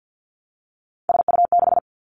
Morse code, heard as a keyed tone, about 50 wpm, 700 Hz. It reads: SU6
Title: Three Challenging Letter-Number Combinations 50wpm